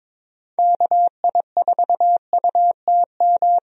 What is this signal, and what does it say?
Morse code, heard as a keyed tone, about 21 wpm, 700 Hz. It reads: KI4UTM
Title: USA Call Signs - 6 Characters - Repeated 3x - 22wpm